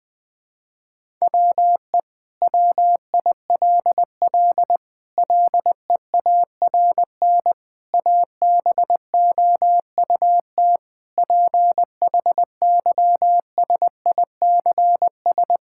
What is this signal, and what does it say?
Morse code, heard as a keyed tone, about 20 wpm, 700 Hz. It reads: WE WILL LEARN ABOUT PHYSICS
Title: Sentences from Top 1000 Words - Review Part 5 - 20wpm